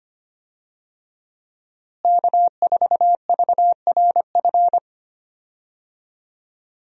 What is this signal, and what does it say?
Morse code, heard as a keyed tone, about 25 wpm, 700 Hz. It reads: K4VRF
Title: USA Call Signs - 5 Characters - Repeated 2x - 25wpm